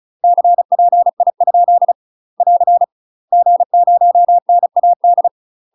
Morse code, heard as a keyed tone, about 36 wpm, 700 Hz. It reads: CPI? <AR> G0NAD
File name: UK MASTER QSOs TRAINING / 35WPM / UK MASTER QSOs 0014